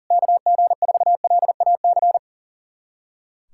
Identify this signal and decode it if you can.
Morse code, heard as a keyed tone, about 37 wpm, 700 Hz. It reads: KG4LAC
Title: USA Callsign DE Callsign - 6 Characters - 40wpm